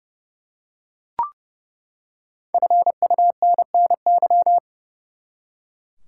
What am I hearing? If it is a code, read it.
Morse code, heard as a keyed tone, about 31 wpm, 700 Hz. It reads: FUNNY
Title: Five Letter Words - 30wpm